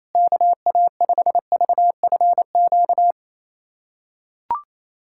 Morse code, heard as a keyed tone, about 28 wpm, 700 Hz. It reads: KA5VFQ
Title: USA Callsign DE Callsign - 6 Characters - 28wpm